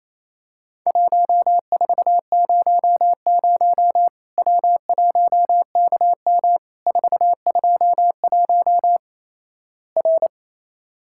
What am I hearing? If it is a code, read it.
Morse code, heard as a keyed tone, about 28 wpm, 700 Hz. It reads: 1400 W1KM 421 R
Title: Worked All Europe DX Contest - QTC Exchange - 28wpm